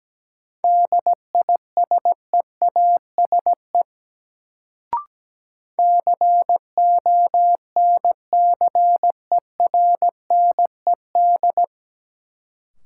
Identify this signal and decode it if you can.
Morse code, heard as a keyed tone, about 17 wpm, 700 Hz. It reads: DISEASE CONCERNED
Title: Sentences from Top 1000 Words - Review Part 6 - New Word Drill - 17wpm